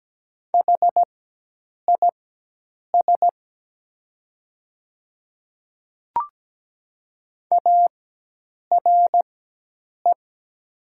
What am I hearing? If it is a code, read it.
Morse code, heard as a keyed tone, about 17 wpm, spaced as for 8 wpm, 700 Hz. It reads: HIS ARE
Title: 4x Character Spacing - Top 100 Words - 17wpm